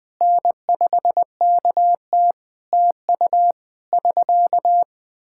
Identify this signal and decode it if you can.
Morse code, heard as a keyed tone, about 21 wpm, 700 Hz. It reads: N5KT TU <SK>
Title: Practice QSO - Exchange 4 - Ending - 20wpm